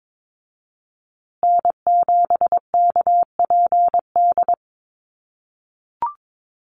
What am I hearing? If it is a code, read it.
Morse code, heard as a keyed tone, about 22 wpm, 700 Hz. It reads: N7KPD
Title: USA Call Signs Repeated 3x 22wpm